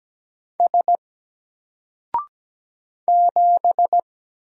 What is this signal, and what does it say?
Morse code, heard as a keyed tone, about 17 wpm, 700 Hz. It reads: S 7
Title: Single Letter-Number 17wpm